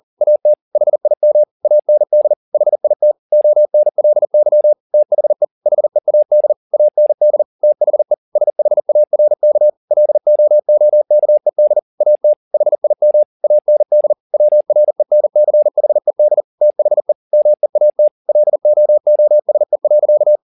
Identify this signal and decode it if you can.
Morse code, heard as a keyed tone, about 40 wpm, 600 Hz. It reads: AT HIM AND HIT ONLY THE HEAD AND THE SHARK LOOKED AT HIM AND WRENCHED THE MEAT LOOSE.
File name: the old man and the sea 2 40wpm